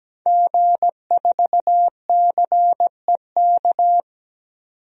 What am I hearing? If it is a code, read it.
Morse code, heard as a keyed tone, about 17 wpm, 700 Hz. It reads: G4CEK
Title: DXCC-100 Call Signs - 5 Characters - 17wpm